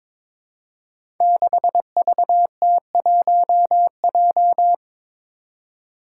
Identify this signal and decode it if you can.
Morse code, heard as a keyed tone, about 22 wpm, 700 Hz. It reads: 6VT1J
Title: Five Challenging Letter-Number Combinations 22wpm